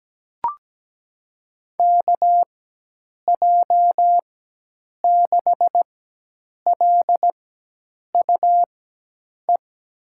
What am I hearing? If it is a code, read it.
Morse code, heard as a keyed tone, about 17 wpm, spaced as for 8 wpm, 700 Hz. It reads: KJ6LUE
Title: USA Call Signs - 4x Character Spacing - 17wpm